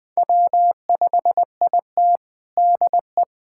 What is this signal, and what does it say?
Morse code, heard as a keyed tone, about 21 wpm, 700 Hz. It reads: W5IT DE
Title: USA Callsign DE Callsign - 4 Characters - 20wpm